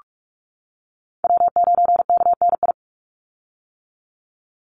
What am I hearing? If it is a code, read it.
Morse code, heard as a keyed tone, about 44 wpm, 700 Hz. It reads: W9KNI
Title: Call Signs - 45wpm